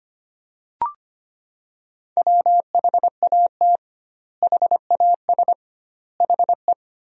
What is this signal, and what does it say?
Morse code, heard as a keyed tone, about 25 wpm, 700 Hz. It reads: WHAT HAS HE
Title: 2x Word-Spacing - Sentences from Top 500 Words - 25wpm